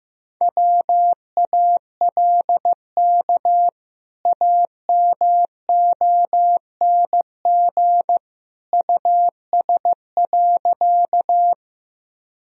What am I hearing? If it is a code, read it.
Morse code, heard as a keyed tone, about 15 wpm, 700 Hz. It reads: WALK AMONG US.
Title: Sentences from Top 1000 Words - Review Part 3 - 15wpm